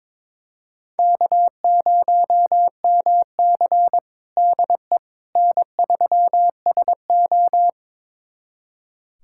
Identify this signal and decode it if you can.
Morse code, heard as a keyed tone, about 22 wpm, 700 Hz. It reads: K0MC DE N3SO
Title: USA Callsign DE Callsign - 4 Characters - 22wpm